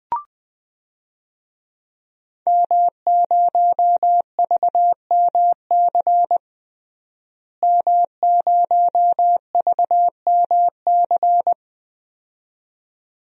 Morse code, heard as a keyed tone, about 20 wpm, 700 Hz. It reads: M0VMC M0VMC
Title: DXCC-100 Call Signs - 5 Characters - Repeated 2x - 20wpm